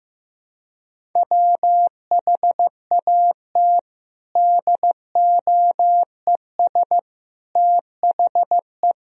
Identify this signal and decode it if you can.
Morse code, heard as a keyed tone, about 15 wpm, 700 Hz. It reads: WHAT DOES THE
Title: Sentences from Top 1000 Words - Review Part 1 - 15wpm